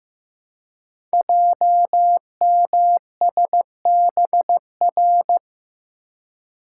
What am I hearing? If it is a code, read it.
Morse code, heard as a keyed tone, about 15 wpm, 700 Hz. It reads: JMSBR
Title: Code Groups - 5 Letters - 15wpm